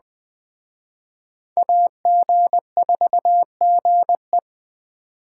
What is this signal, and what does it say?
Morse code, heard as a keyed tone, about 21 wpm, 700 Hz. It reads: AG4GE
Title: Speed Racing - USA 2x2 Call Signs 17wpm